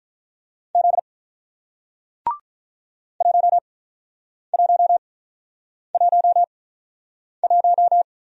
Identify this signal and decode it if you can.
Morse code, heard as a keyed tone, about 48 wpm, spaced as for 9 wpm, 700 Hz. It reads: Z 1111